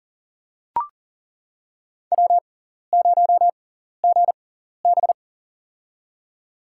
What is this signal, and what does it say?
Morse code, heard as a keyed tone, about 40 wpm, spaced as for 14 wpm, 700 Hz. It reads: W0GB
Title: USA Call Signs - 6x Character Spacing - 40wpm